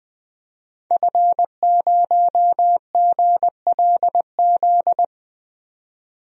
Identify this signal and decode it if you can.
Morse code, heard as a keyed tone, about 20 wpm, 700 Hz. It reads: F0GLZ
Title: DXCC-100 Call Signs - 5 Characters - 20wpm